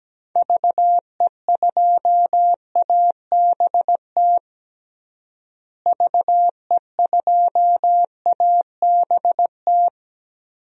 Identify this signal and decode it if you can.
Morse code, heard as a keyed tone, about 17 wpm, 700 Hz. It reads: VE2ABT VE2ABT
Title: DXCC-100 Call Signs - 6 Characters - Repeated 3x - 17wpm